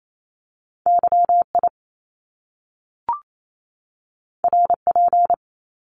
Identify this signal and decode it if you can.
Morse code, heard as a keyed tone, about 28 wpm, 700 Hz. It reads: YI RP
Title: Top Two Letter Combinations in Words 28wpm